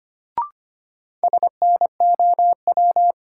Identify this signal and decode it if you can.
Morse code, heard as a keyed tone, about 25 wpm, 700 Hz. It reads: SNOW
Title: Sentences from Top 500 Words - New Word Drill - 25wpm